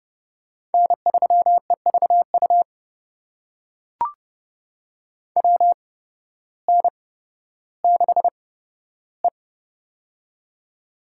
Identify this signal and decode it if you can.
Morse code, heard as a keyed tone, about 30 wpm, spaced as for 14 wpm, 700 Hz. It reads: N3EVU WN6E
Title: USA Call Signs - 8x Character Spacing - 30wpm